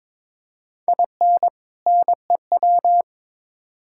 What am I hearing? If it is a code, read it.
Morse code, heard as a keyed tone, about 22 wpm, 700 Hz. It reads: IN NEW
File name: Build A Phrase - 2-Word Phrases - 22wpm